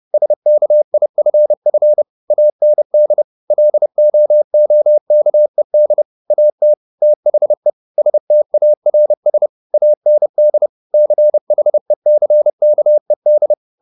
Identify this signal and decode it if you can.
Morse code, heard as a keyed tone, about 30 wpm, 600 Hz. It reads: SKIFF AND LOOKED AT THE STARS AND CHECKED
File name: the old man and the sea / the old man and the sea 1 30wpm